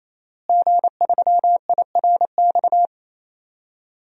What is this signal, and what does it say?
Morse code, heard as a keyed tone, about 28 wpm, 700 Hz. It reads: G3IRX
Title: DXCC-100 Call Signs - 5 Characters - Repeated 3x - 28wpm